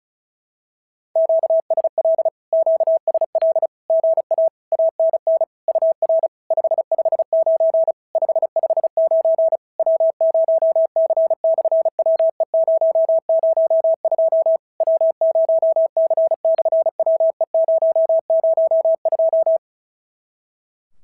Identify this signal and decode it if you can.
Morse code, heard as a keyed tone, about 36 wpm, 650 Hz. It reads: QSL QSL GA ANN UR 559 559 W0C/WE002 W0C/WE002
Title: SOTA QSO Practice 35wpm